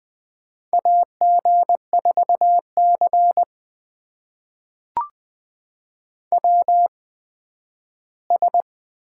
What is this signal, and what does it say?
Morse code, heard as a keyed tone, about 20 wpm, spaced as for 12 wpm, 700 Hz. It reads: AG4C WS
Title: USA Call Signs - 4 Characters - 8x Character Spacing - 20wpm